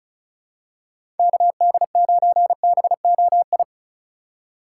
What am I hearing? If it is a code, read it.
Morse code, heard as a keyed tone, about 36 wpm, 690 Hz. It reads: KD9BOI
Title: USA Call Signs - 6 Characters - 4x Character Spacing - 35wpm